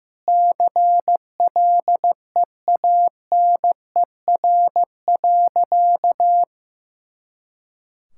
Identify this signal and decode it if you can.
Morse code, heard as a keyed tone, about 15 wpm, 700 Hz. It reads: CLEANER.
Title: Pangram Challenge v2 - 15wpm